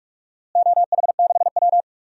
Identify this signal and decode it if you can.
Morse code, heard as a keyed tone, about 46 wpm, 700 Hz. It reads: OH6W
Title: DXCC-100 Call Signs - 4 Characters - Repeated 2x - 45wpm